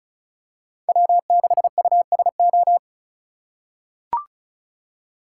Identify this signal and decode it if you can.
Morse code, heard as a keyed tone, about 36 wpm, 700 Hz. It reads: W6USO
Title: USA Call Signs - 2x Character Spacing - 35wpm